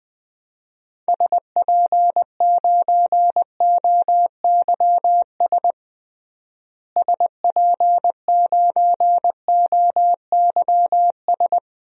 Morse code, heard as a keyed tone, about 20 wpm, 700 Hz. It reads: SP9OYS SP9OYS
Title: DXCC-100 Call Signs - 6 Characters - Repeated 2x - 20wpm